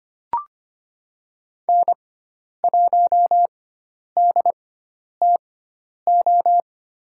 Morse code, heard as a keyed tone, about 25 wpm, spaced as for 10 wpm, 700 Hz. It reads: N1DTO
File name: Lesson 16g - Letter c - Call Signs 25wpm